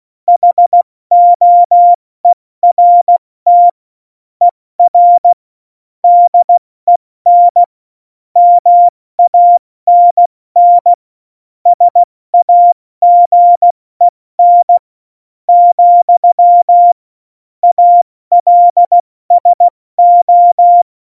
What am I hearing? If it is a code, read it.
Morse code, heard as a keyed tone, about 16 wpm, spaced as for 14 wpm, 700 Hz. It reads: HOERT ER DEN MANN SAGEN, ALSO
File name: Morsekurs / Witze / Witz 13. Oktober 2020, 70 F80